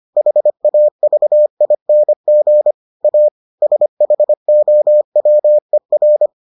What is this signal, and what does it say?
Morse code, heard as a keyed tone, about 25 wpm, 600 Hz. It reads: HAVING A SHOWER